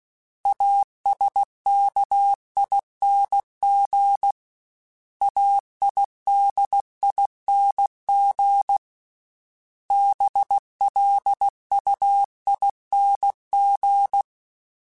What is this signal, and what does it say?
Morse code, heard as a keyed tone, about 16 wpm, 800 Hz. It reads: ASKING AIDING BLUING
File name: Special INGs / Special INGs-mp3s / Sngl ing @15